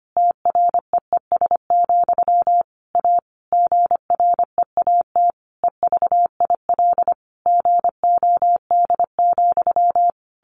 Morse code, heard as a keyed tone, about 25 wpm, 700 Hz. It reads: TREES, A GREAT EVIL GOD,
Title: A Sound Of Thunder 25wpm